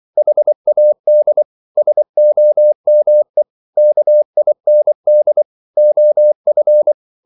Morse code, heard as a keyed tone, about 24 wpm, 600 Hz. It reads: HAD SOME KIND OF